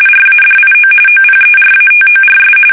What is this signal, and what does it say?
RTTY, about 45.45 baud. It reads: RADIO OPERATORS
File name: fsk850 / teletype-RTTY